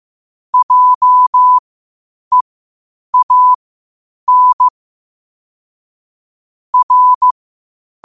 Morse code, heard as a keyed tone, about 15 wpm, spaced as for 8 wpm, 1000 Hz. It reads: JEAN R
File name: qotdfiles / 8wpm